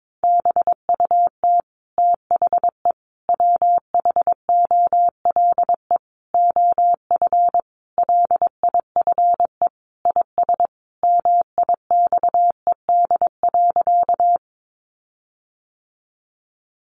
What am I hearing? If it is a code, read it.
Morse code, heard as a keyed tone, about 22 wpm, 700 Hz. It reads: BUT THE WHOLE OF LIFE IS MIXED.
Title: Sentences from Top 1500 Words 22wpm